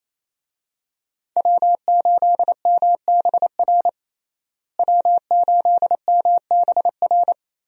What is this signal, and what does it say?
Morse code, heard as a keyed tone, about 27 wpm, 700 Hz. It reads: W8MBR W8MBR
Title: USA Call Signs - 5 Characters - Repeated 2x - 28wpm